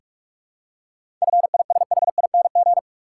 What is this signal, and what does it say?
Morse code, heard as a keyed tone, about 46 wpm, 700 Hz. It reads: FISHING
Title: Seven Letter Words - 45wpm